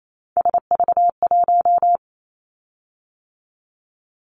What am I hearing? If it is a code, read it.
Morse code, heard as a keyed tone, about 28 wpm, 700 Hz. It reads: SV1
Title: Callsign Prefixes for Top 50 DXCC - 28wpm